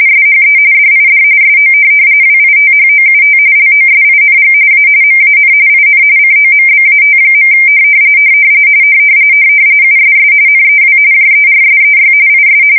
RTTY, about 45.45 baud. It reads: SHE DID SO WITHOUT LOSING A MAN IN COMBAT. USS NEW JERSEY IS
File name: fsk170 / USS-New-Jersey